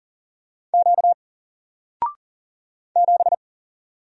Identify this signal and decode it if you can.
Morse code, heard as a keyed tone, about 40 wpm, 700 Hz. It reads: Q 7
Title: Single Letter-Number 40wpm